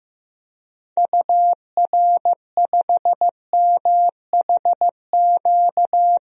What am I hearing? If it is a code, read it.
Morse code, heard as a keyed tone, about 15 wpm, 700 Hz. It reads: UR5MHQ
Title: DXCC-100 Call Signs - 6 Characters - Repeated 3x - 15wpm